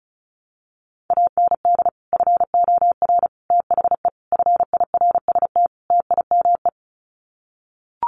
Morse code, heard as a keyed tone, about 34 wpm, 700 Hz. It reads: AND FOR THE FIRST TIME
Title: Sets of 5 Words 35wpm